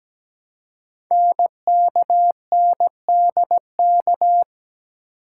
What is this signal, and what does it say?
Morse code, heard as a keyed tone, about 17 wpm, 700 Hz. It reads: NKNDK
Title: Code Groups - 5 Letters - 17wpm